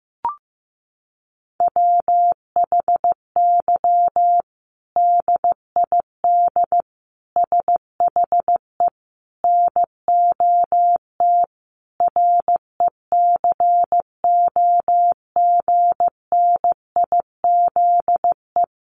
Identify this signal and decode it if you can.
Morse code, heard as a keyed tone, about 15 wpm, 700 Hz. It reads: WHY DID SHE NOT RECOGNIZE
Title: Sentences from Top 1400 Words 15wpm